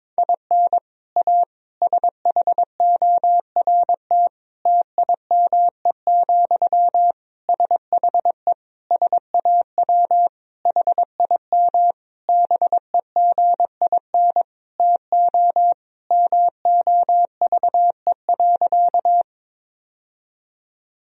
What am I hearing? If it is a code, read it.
Morse code, heard as a keyed tone, about 21 wpm, 700 Hz. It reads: IN A SHORT TIME, SHE SAW HIM BEGIN TO MOVE.
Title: Sentences from Top 1000 Words - Review Part 2 - 22wpm